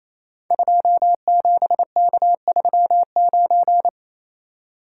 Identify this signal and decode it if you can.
Morse code, heard as a keyed tone, about 27 wpm, 700 Hz. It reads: 27K39
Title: Code Groups - 5 Letter:Numbers - 28wpm